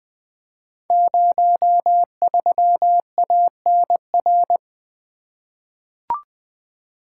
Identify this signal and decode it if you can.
Morse code, heard as a keyed tone, about 20 wpm, 700 Hz. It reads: 03ANR
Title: Code Groups - 5 Letter:Numbers - 20wpm